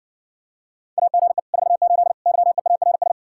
Morse code, heard as a keyed tone, about 60 wpm, 700 Hz. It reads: AGE 48 YEARS